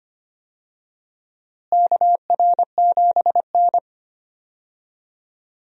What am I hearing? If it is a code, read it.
Morse code, heard as a keyed tone, about 25 wpm, 700 Hz. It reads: KR7N
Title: USA Call Signs - 4 Characters - Repeated 2x - 25wpm